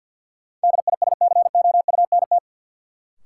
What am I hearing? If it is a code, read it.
Morse code, heard as a keyed tone, about 53 wpm, 700 Hz. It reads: DISCOUNT
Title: Sentences from Top 1200 Words - New Word Drill Alt - 50wpm